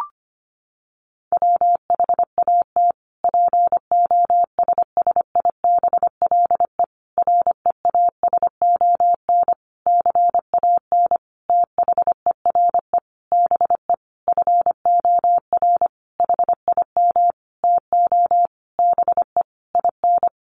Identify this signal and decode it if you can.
Morse code, heard as a keyed tone, about 25 wpm, 700 Hz. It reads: WHAT POSSIBLE REASON CAN THERE BE FOR HIM TO BE IN
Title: Sentences from Top 1300 Words 25wpm